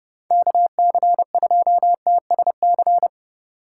Morse code, heard as a keyed tone, about 28 wpm, 700 Hz. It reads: KC2TSC